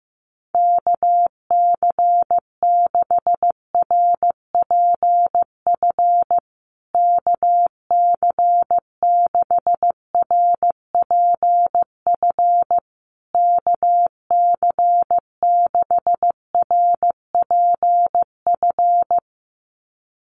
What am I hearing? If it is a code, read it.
Morse code, heard as a keyed tone, about 15 wpm, 700 Hz. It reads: KC6RPF KC6RPF KC6RPF
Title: USA Call Signs Repeated 3x 15wpm